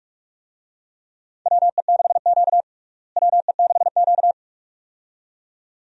Morse code, heard as a keyed tone, about 44 wpm, 700 Hz. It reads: WE6Q WE6Q
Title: USA Call Signs - 4 Characters - Repeated 2x - 45wpm